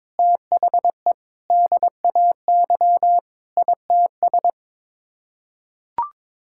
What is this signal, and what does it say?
Morse code, heard as a keyed tone, about 22 wpm, 700 Hz. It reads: THE DAY ITS
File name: practice / 22wpm / Sets of 8 Words - 22wpm